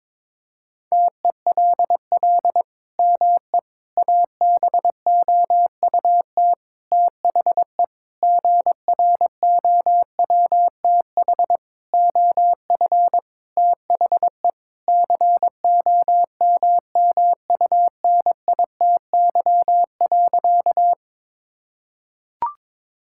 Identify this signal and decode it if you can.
Morse code, heard as a keyed tone, about 22 wpm, 700 Hz. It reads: TELL ME ABOUT THE GROWTH OF THE COMMUNITY.
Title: Sentences from Top 1000 Words - Review Part 5 - 22wpm